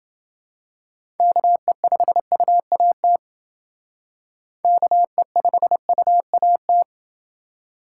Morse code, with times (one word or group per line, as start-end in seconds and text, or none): 1.2-3.2 KE5UAT
4.6-6.8 KE5UAT